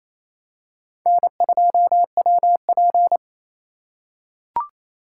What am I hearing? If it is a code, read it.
Morse code, heard as a keyed tone, about 28 wpm, 700 Hz. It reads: N2WP
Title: USA Call Signs - 4 Characters - 2x Character Spacing - 28wpm